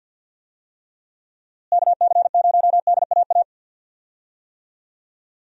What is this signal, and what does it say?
Morse code, heard as a keyed tone, about 49 wpm, 700 Hz. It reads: KC0DAA
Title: USA Call Signs - 6 Characters - Repeated 3x - 50wpm